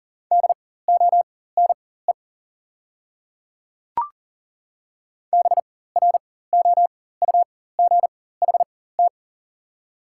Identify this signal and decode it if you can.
Morse code, heard as a keyed tone, about 40 wpm, spaced as for 19 wpm, 690 Hz. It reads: DONE BROUGHT
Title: 4x Character Spacing - Top 500 Words - 40wpm